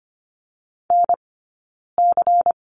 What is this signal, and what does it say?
Morse code, heard as a keyed tone, about 25 wpm, 700 Hz. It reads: N C
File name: course / 25:10wpm / 092 Lesson 16d - Letter c - Two Characters 25wpm